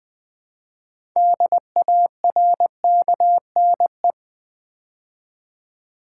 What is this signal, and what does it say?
Morse code, heard as a keyed tone, about 20 wpm, 700 Hz. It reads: DARKNE
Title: Build A Word - 2 Letters At A Time - 8-Letter Words - 20wpm